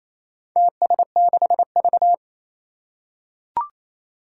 Morse code, heard as a keyed tone, about 28 wpm, 700 Hz. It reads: TS6V
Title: Four Challenging Letter-Number Combinations 28wpm